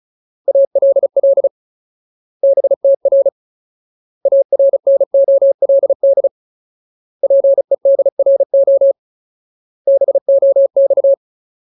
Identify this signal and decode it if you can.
Morse code, heard as a keyed tone, about 35 wpm, 550 Hz. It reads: ALL BTR ARNOLD PEDRO BOX
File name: Ultimate Word List / 35WPM Ultimate